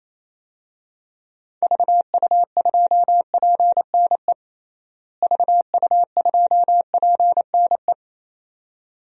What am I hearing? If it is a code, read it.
Morse code, heard as a keyed tone, about 28 wpm, 700 Hz. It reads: VU2PNE VU2PNE
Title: DXCC-100 Call Signs - 6 Characters - Repeated 2x - 28wpm